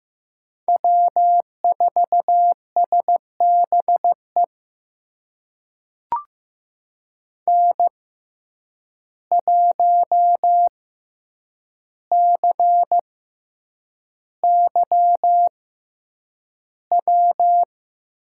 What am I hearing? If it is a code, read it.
Morse code, heard as a keyed tone, about 15 wpm, spaced as for 8 wpm, 700 Hz. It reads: W4SBE N1CY W
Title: USA Call Signs - 5 Characters - 6x Character Spacing - 15wpm